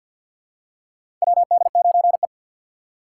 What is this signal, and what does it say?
Morse code, heard as a keyed tone, about 46 wpm, 700 Hz. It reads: WD9E
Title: USA Call Signs - 4 Characters - 2x Character Spacing - 50wpm